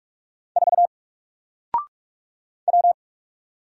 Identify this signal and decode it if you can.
Morse code, heard as a keyed tone, about 46 wpm, 700 Hz. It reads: 4 W
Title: Single Letter-Number 45wpm